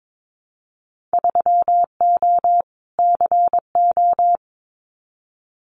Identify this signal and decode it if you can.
Morse code, heard as a keyed tone, about 22 wpm, 700 Hz. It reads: 3O CO
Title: Winter Field Day Exchanges Encore 22wpm